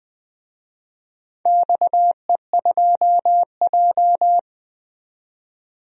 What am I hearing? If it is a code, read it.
Morse code, heard as a keyed tone, about 20 wpm, 700 Hz. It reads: XE2J
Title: DXCC-100 Call Signs - 4 Characters - 20wpm